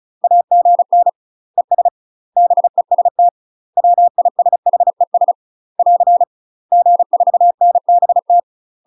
Morse code, heard as a keyed tone, about 36 wpm, 700 Hz. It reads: AGN ES BEST WISHES <AR> G4NBT
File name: UK MASTER QSOs TRAINING / 35WPM / UK MASTER QSOs 0034